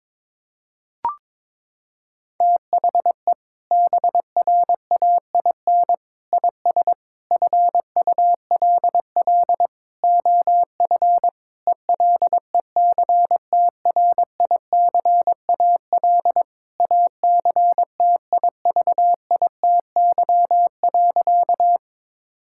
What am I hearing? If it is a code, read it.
Morse code, heard as a keyed tone, about 22 wpm, 700 Hz. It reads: THE BRAIN IS FULL OF ELECTRICAL ACTIVITY.
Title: Sentences from Top 1000 Words - Review Part 6 - 22wpm